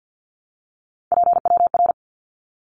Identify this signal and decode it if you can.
Morse code, heard as a keyed tone, about 49 wpm, 700 Hz. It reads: PWR
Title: Abbreviations 50wpm